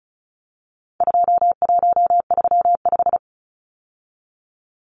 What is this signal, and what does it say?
Morse code, heard as a keyed tone, about 34 wpm, 700 Hz. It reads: 2135
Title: Numbers - 4-Digit 35wpm